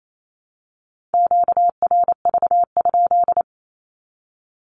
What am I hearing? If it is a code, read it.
Morse code, heard as a keyed tone, about 28 wpm, 700 Hz. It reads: QRV?
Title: Q-Codes 28wpm